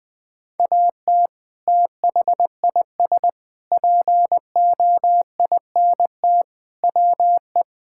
Morse code, heard as a keyed tone, about 20 wpm, 700 Hz. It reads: AT THIS POINT WE
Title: Sentences with 3-Word Phrases - 20wpm